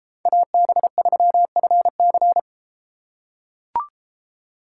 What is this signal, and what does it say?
Morse code, heard as a keyed tone, about 30 wpm, 700 Hz. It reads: AB3FC